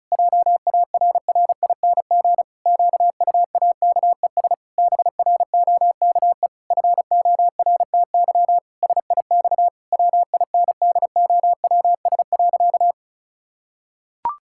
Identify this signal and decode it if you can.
Morse code, heard as a keyed tone, about 36 wpm, 700 Hz. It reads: JARRING QUAKES BROKE FORTY SIX WINDOWS.
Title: Pangram Challenge v2 - 35wpm